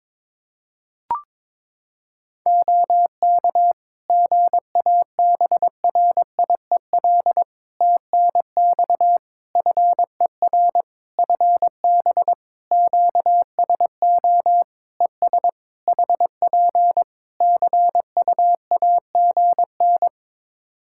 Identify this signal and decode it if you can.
Morse code, heard as a keyed tone, about 22 wpm, 700 Hz. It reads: OK GABRIEL TNX FER FB QSO ES HP CUAGN
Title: Practice QSO - Exchange 4 - Ending - 22wpm